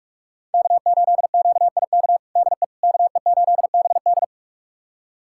Morse code, heard as a keyed tone, about 41 wpm, 700 Hz. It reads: K8QIK DE KE8BD
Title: USA Callsign DE Callsign - 5 Characters - 45wpm